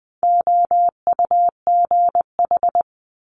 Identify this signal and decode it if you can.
Morse code, heard as a keyed tone, about 20 wpm, 700 Hz. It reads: OUGH
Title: Multi-letter Phonemes Quick Fire 20wpm